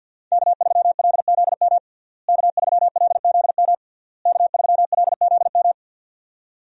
Morse code, heard as a keyed tone, about 46 wpm, 690 Hz. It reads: K3LZM K3LZM K3LZM
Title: USA Call Signs - 5 Characters - Repeated 3x - 50wpm